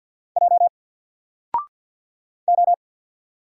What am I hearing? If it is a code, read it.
Morse code, heard as a keyed tone, about 48 wpm, 700 Hz. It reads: J O
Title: Single Character (Letter:Number:Punctuation:Prosign) 50wpm